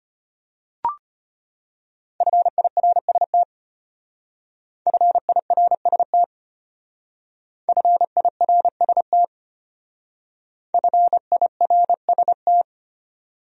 Morse code, heard as a keyed tone, about 32 wpm, 700 Hz. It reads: FIRST FIRST FIRST FIRST